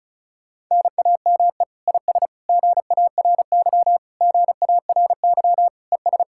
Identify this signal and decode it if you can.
Morse code, heard as a keyed tone, about 40 wpm, 700 Hz. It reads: NAME IS GARY GARY ES